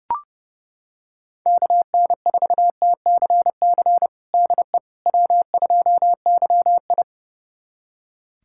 Morse code, heard as a keyed tone, about 30 wpm, 700 Hz. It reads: KN4TCC DE W2YI
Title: USA Callsign DE Callsign 30wpm